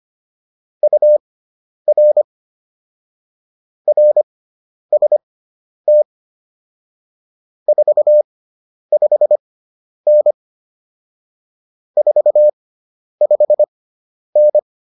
Morse code, heard as a keyed tone, about 25 wpm, spaced as for 10 wpm, 600 Hz. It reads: UR RST 45N 45N
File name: QSO104 10